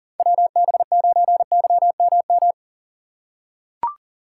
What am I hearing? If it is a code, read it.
Morse code, heard as a keyed tone, about 40 wpm, 700 Hz. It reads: WB9YMM
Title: USA Call Signs - 6 Characters - 3x Character Spacing - 40wpm